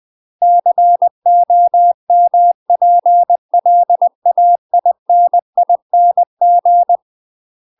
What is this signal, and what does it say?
Morse code, heard as a keyed tone, about 20 wpm, 700 Hz. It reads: COMPLAINING
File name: UK MASTER QSOs TRAINING / 20WPM / UK MASTER QSOs 0007